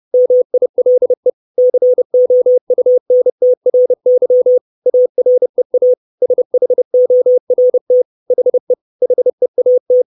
Morse code, heard as a keyed tone, about 30 wpm, 500 Hz. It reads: MILE COUNTRY AREA SHORT HE HEAT
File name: 632 Words 30WPM